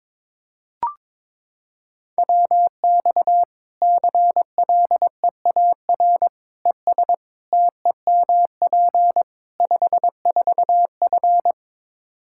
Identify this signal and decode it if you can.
Morse code, heard as a keyed tone, about 22 wpm, 700 Hz. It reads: WX CLEAR ES TEMP 54F
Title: Practice QSO - Exchange 2 - Info - 22wpm